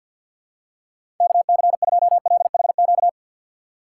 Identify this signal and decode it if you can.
Morse code, heard as a keyed tone, about 53 wpm, 700 Hz. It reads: KC2LHQ
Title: USA Call Signs - 6 Characters - 8x Character Spacing - 50wpm